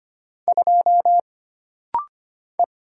Morse code, heard as a keyed tone, about 25 wpm, 700 Hz. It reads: 2 E
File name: Single Character (Letter:Number:Punctuation:Prosign) - Rapid-Fire - 25wpm